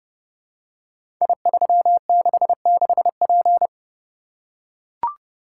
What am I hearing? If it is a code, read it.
Morse code, heard as a keyed tone, about 28 wpm, 700 Hz. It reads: I366P
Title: Code Groups - 5 Letter:Numbers - 30wpm